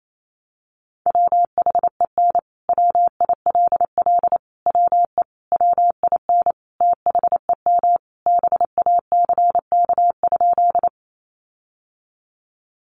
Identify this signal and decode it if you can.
Morse code, heard as a keyed tone, about 28 wpm, 700 Hz. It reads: WHEN WILL WE WIN THEM BACK?
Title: Sentences from Top 1500 Words 28wpm